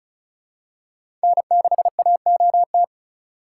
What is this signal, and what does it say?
Morse code, heard as a keyed tone, about 34 wpm, 700 Hz. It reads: N6AOT
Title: USA Call Signs - 5 Characters - 2x Character Spacing - 35wpm